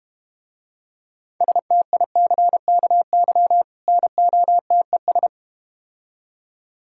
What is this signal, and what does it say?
Morse code, heard as a keyed tone, about 32 wpm, 700 Hz. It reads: STICKY NOTES